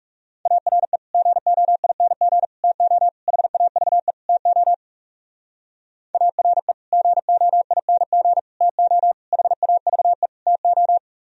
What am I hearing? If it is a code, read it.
Morse code, heard as a keyed tone, about 40 wpm, 700 Hz. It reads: ARE GOING TO HAVE TO ARE GOING TO HAVE TO